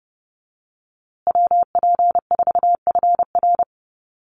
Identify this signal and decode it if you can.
Morse code, heard as a keyed tone, about 29 wpm, 700 Hz. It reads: WP4FR
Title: USA Call Signs Repeated 2x 30wpm